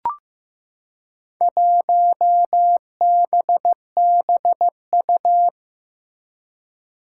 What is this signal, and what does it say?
Morse code, heard as a keyed tone, about 15 wpm, 700 Hz. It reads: 1BBU
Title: Four Challenging Letter-Number Combinations 15wpm